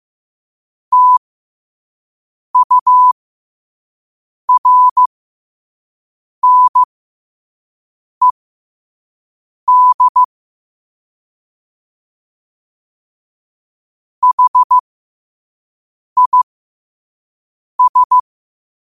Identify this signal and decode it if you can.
Morse code, heard as a keyed tone, about 15 wpm, spaced as for 5 wpm, 1000 Hz. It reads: TURNED HIS
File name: qotdfiles / 5wpm